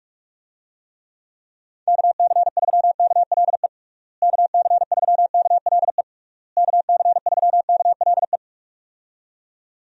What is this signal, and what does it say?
Morse code, heard as a keyed tone, about 46 wpm, 700 Hz. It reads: KC3KLE KC3KLE KC3KLE
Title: USA Call Signs - 6 Characters - Repeated 3x - 45wpm